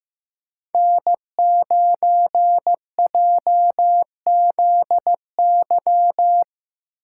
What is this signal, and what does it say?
Morse code, heard as a keyed tone, about 15 wpm, 700 Hz. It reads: N9JZY
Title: K1USN SST QSO Practice - 15wpm